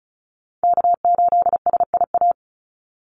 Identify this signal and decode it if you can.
Morse code, heard as a keyed tone, about 34 wpm, 700 Hz. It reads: K8SIA
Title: Call Signs Repeated 2x - 35wpm